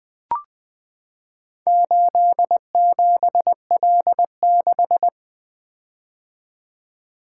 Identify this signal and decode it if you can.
Morse code, heard as a keyed tone, about 20 wpm, 700 Hz. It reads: 87L6
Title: Four Letter-Number Combinations - 20wpm